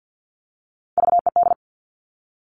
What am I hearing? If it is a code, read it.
Morse code, heard as a keyed tone, about 50 wpm, 700 Hz. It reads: VED
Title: Top Three Letter Combinations in Words 50wpm